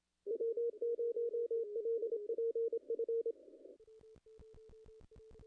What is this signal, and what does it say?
Morse code, heard as a keyed tone, about 26 wpm, 440 Hz, with radio noise. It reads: W0LPF
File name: contests / ss-cw-k5tr-op-n5rz -0000